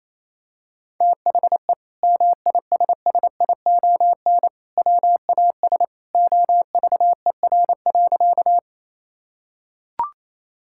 Morse code, heard as a keyed tone, about 28 wpm, 700 Hz. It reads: THE MISSION WAS OVER.
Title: Sentences from Top 1600 Words 28wpm